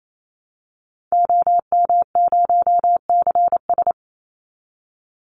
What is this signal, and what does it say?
Morse code, heard as a keyed tone, about 28 wpm, 700 Hz. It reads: OM0CS
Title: Call Signs Repeated 3x - 28wpm